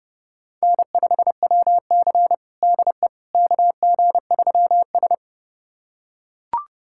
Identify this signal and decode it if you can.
Morse code, heard as a keyed tone, about 30 wpm, 700 Hz. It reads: N5WC DE KG3S
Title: USA Callsign DE Callsign - 4 Characters - 30wpm